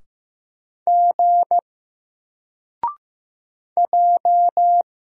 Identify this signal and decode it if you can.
Morse code, heard as a keyed tone, about 15 wpm, 700 Hz. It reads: G J
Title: Single Letter-Number 15wpm